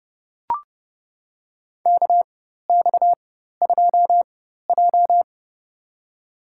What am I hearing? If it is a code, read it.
Morse code, heard as a keyed tone, about 31 wpm, spaced as for 14 wpm, 700 Hz. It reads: KX2J
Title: USA Call Signs - 4 Characters - 4x Character Spacing - 30wpm